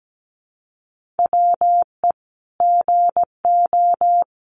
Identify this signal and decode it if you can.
Morse code, heard as a keyed tone, about 17 wpm, 700 Hz. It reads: WE GO
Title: Sentences from Top 1200 Words 17wpm